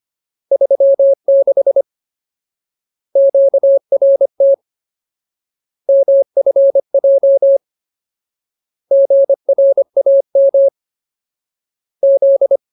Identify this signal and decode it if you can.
Morse code, heard as a keyed tone, about 25 wpm, 550 Hz. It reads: 36 QRT MFJ GRAM Z
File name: Ultimate Word List / 25WPM Ultimate